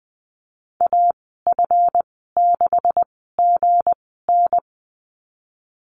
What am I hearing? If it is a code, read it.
Morse code, heard as a keyed tone, about 20 wpm, spaced as for 15 wpm, 700 Hz. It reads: AF6GN
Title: 179 Lesson 29g - Slash - Call Signs 20wpm